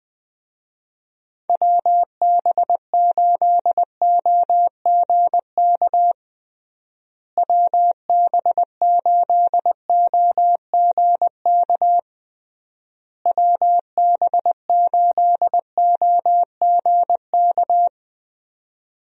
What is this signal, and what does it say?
Morse code, heard as a keyed tone, about 20 wpm, 700 Hz. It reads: WB8OGK WB8OGK WB8OGK
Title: USA Call Signs - 6 Characters - Repeated 3x - 20wpm